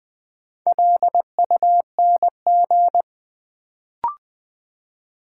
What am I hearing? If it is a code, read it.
Morse code, heard as a keyed tone, about 20 wpm, 700 Hz. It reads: LUNG
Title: Build A Word - 2 Letters At A Time - 4-Letter Words - 20wpm